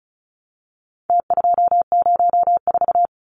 Morse code, heard as a keyed tone, about 34 wpm, 700 Hz. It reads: T204
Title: Numbers - Contest Style - 4-Digit 35wpm